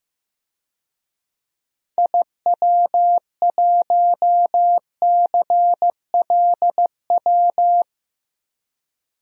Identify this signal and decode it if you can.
Morse code, heard as a keyed tone, about 15 wpm, 700 Hz. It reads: IW1CLW
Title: DXCC-100 Call Signs - 6 Characters - 15wpm